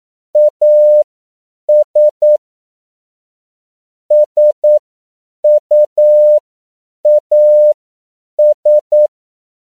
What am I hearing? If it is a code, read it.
Morse code, heard as a keyed tone, about 9 wpm, spaced as for 7 wpm, 600 Hz. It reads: AS SUAS
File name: CW / 10PPM